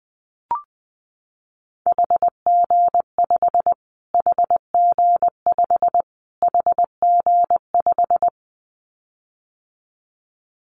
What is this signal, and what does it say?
Morse code, heard as a keyed tone, about 19 wpm, 700 Hz. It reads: HG5 HG5 HG5
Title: Callsign Prefixes for Top 25 DXCC - Repeated 3x - 20wpm